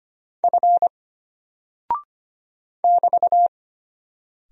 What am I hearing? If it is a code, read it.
Morse code, heard as a keyed tone, about 25 wpm, 700 Hz. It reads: F <BT>
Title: Single Character (Letter:Number:Punctuation:Prosign) Alt 25wpm